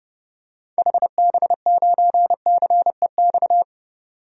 Morse code, heard as a keyed tone, about 30 wpm, 700 Hz. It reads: HB9CEX
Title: DXCC-100 Call Signs - 6 Characters - 30wpm